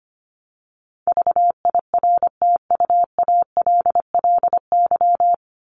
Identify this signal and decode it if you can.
Morse code, heard as a keyed tone, about 25 wpm, 700 Hz. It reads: VIRTUALLY
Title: Top 1501-1750 Words 25wpm